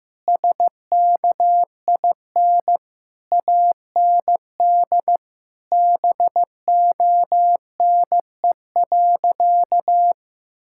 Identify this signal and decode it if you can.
Morse code, heard as a keyed tone, about 15 wpm, 700 Hz. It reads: SKIN AND BONE.
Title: Sentences from Top 1600 Words 15wpm